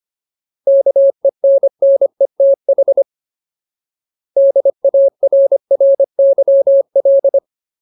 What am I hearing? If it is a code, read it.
Morse code, heard as a keyed tone, about 25 wpm, 550 Hz. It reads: KENNETH DARRYL
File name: Ultimate Word List / 25WPM Ultimate